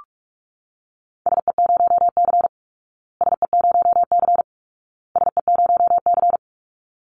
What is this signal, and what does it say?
Morse code, heard as a keyed tone, about 44 wpm, 700 Hz. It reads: SE0C SE0C SE0C
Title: Call Signs Repeated 2x - 45wpm